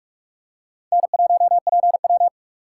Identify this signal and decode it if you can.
Morse code, heard as a keyed tone, about 44 wpm, 700 Hz. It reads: N1PW